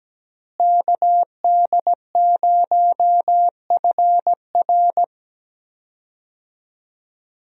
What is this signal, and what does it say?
Morse code, heard as a keyed tone, about 17 wpm, 700 Hz. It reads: KD0FR
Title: USA Call Signs - 5 Characters - Repeated 3x - 17wpm